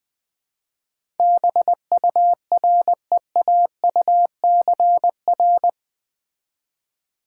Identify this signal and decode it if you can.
Morse code, heard as a keyed tone, about 20 wpm, 700 Hz. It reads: BUREAUCR
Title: Build A Word - 4 Letters At A Time - 11-Letter Words - 20wpm